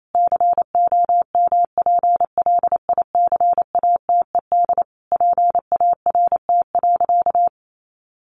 Morse code, heard as a keyed tone, about 28 wpm, 700 Hz. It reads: COMPLICATED PART.
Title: Sentences from Top 1200 Words 28wpm